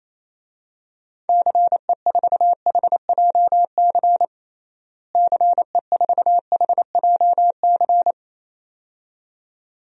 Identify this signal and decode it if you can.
Morse code, heard as a keyed tone, about 28 wpm, 700 Hz. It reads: CE4HJC CE4HJC
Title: DXCC-100 Call Signs - 6 Characters - Repeated 2x - 28wpm